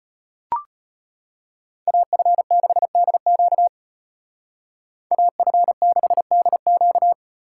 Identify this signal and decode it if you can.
Morse code, heard as a keyed tone, about 30 wpm, 700 Hz. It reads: AF6DQ AF6DQ
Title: Speed Racing - USA 2x2 Call Signs 25wpm